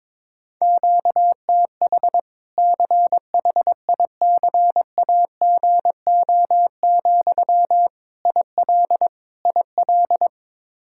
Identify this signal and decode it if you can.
Morse code, heard as a keyed tone, about 22 wpm, 700 Hz. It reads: QTH CHICAGO, IL IL
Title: Practice QSO - Exchange 1 - Beginning - 22wpm